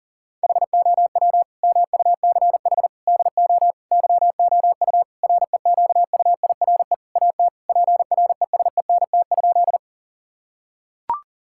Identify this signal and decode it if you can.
Morse code, heard as a keyed tone, about 40 wpm, 700 Hz. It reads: HOW MUCH DO YOU REQUIRE AT PRESENT?
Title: Sentences from Top 1000 Words - Review Part 6 - 40wpm